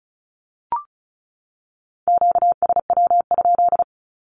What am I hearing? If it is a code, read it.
Morse code, heard as a keyed tone, about 34 wpm, 700 Hz. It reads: QSW?
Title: Q-Codes 35wpm